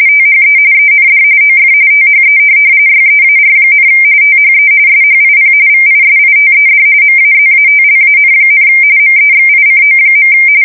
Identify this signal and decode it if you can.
RTTY, about 45.45 baud. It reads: THE END OF THE YEAR, THE SHIP VISITED MAZATLAN, MEXICO.
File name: fsk170 / USS-Missouri